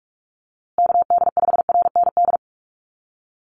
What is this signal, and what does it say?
Morse code, heard as a keyed tone, about 44 wpm, 700 Hz. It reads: KD5RND
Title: USA Call Signs Repeated 2x 45wpm